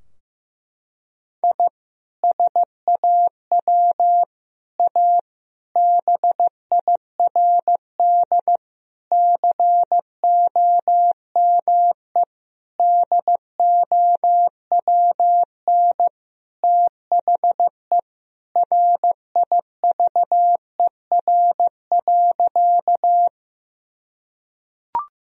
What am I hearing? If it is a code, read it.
Morse code, heard as a keyed tone, about 15 wpm, 700 Hz. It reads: I SAW A BIRD COME DOWN THE RIVER.
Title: Sentences from Top 1000 Words - Review Part 2 - 15wpm